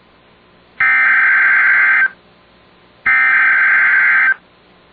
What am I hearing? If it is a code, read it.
Emergency Alert System (SAME) tones (header bursts), heard as harsh buzzing bursts. It is heard from 0.8 to 4.3 s.